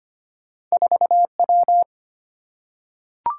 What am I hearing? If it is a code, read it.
Morse code, heard as a keyed tone, about 25 wpm, 690 Hz. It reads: 4W
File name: Two Letter-Number Combinations - 25wpm